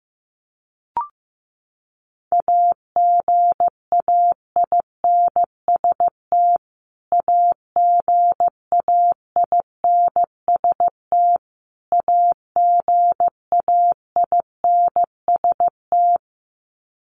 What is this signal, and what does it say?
Morse code, heard as a keyed tone, about 15 wpm, 700 Hz. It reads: AGAINST AGAINST AGAINST
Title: Top 500 Words Repeated 15wpm